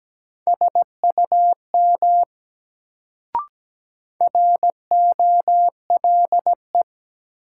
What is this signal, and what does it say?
Morse code, heard as a keyed tone, about 17 wpm, 700 Hz. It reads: SUM ROLE
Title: Sentences from Top 1000 Words - Review Part 6 - New Word Drill - 17wpm